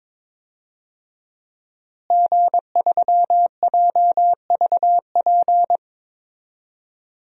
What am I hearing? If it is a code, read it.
Morse code, heard as a keyed tone, about 22 wpm, 700 Hz. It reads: G3JVP
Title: DXCC-100 Call Signs - 5 Characters - Repeated 2x - 22wpm